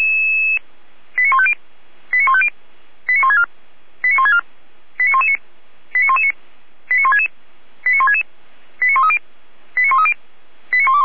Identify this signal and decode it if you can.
Five-tone selective call (ZVEI), heard as quick runs of five tones, tones of about 70 ms. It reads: 89150 89150 89164 89164 89119 89119 89166 89166 89130 89130 89121